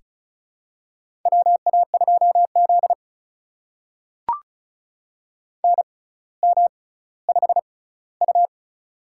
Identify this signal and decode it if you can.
Morse code, heard as a keyed tone, about 36 wpm, spaced as for 18 wpm, 700 Hz. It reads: WA2Z NM5U
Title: USA Call Signs - 4 Characters - 6x Character Spacing - 35wpm